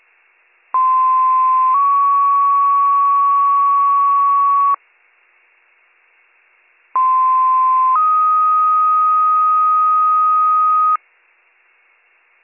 Two-tone page, heard as a one-second tone, then a three-second tone, about 1040 then 1120 Hz, about 1040 then 1250 Hz. It is heard from 0.7 to 11.0 s.